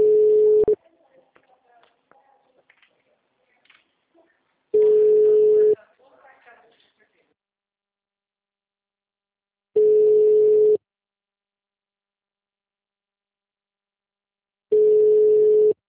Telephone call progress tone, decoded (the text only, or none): ringback tone